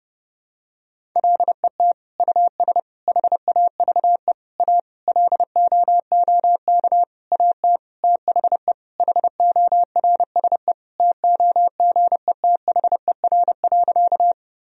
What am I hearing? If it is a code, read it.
Morse code, heard as a keyed tone, about 32 wpm, 700 Hz. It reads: LET US HAVE A LOOK AT THE HORSE TOGETHER.
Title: Sentences from Top 1000 Words - Review Part 1 - 30wpm